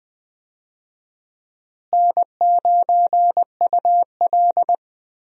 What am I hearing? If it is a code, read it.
Morse code, heard as a keyed tone, about 20 wpm, 700 Hz. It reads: N9UL
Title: USA Call Signs - 4 Characters - Repeated 3x - 20wpm